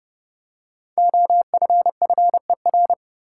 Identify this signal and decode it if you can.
Morse code, heard as a keyed tone, about 32 wpm, 700 Hz. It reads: OFFER
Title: Five Letter Words - 30wpm